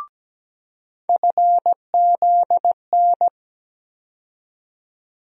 Morse code, heard as a keyed tone, about 17 wpm, 700 Hz. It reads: FZN
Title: USA Call Sign Postfixes - 3 Letters 17wpm